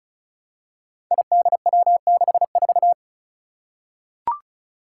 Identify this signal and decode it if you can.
Morse code, heard as a keyed tone, about 36 wpm, 700 Hz. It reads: IDW64
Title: Five Challenging Letter-Number Combinations 35wpm